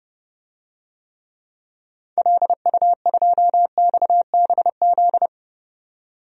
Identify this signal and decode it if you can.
Morse code, heard as a keyed tone, about 32 wpm, 700 Hz. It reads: LU2XBZ
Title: DXCC-100 Call Signs - 6 Characters - 30wpm